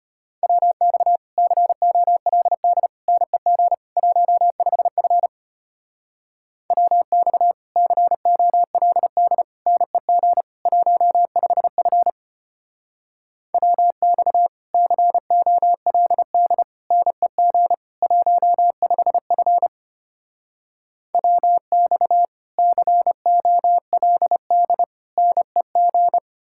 Morse code, heard as a keyed tone, about 32 wpm, 700 Hz. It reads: WX COLD NEG 15F WX COLD NEG 15F WX COLD NEG 15F WX COLD NEG